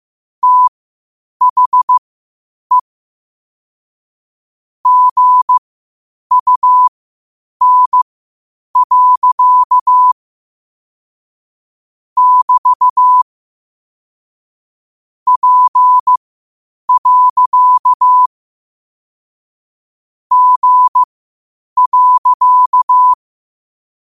Morse code, heard as a keyed tone, about 15 wpm, spaced as for 8 wpm, 1000 Hz. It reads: THE GUN. <BT> P. G.
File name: qotdfiles / 8wpm